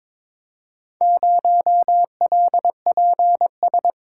Morse code, heard as a keyed tone, about 22 wpm, 700 Hz. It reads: 0LPS
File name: practice / 22wpm / Four Letter-Number Combinations - 22wpm